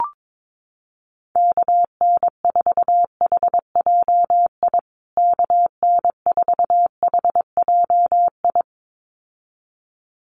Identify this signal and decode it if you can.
Morse code, heard as a keyed tone, about 22 wpm, 700 Hz. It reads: KN4HJI KN4HJI
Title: USA Call Signs Repeated 2x 22wpm